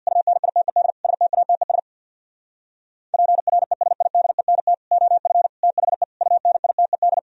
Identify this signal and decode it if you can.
Morse code, heard as a keyed tone, about 61 wpm, 700 Hz. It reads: UNITED STATES PRESIDENT OF THE UNITED
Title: Speed-Racing - Sets of 5 Words 50wpm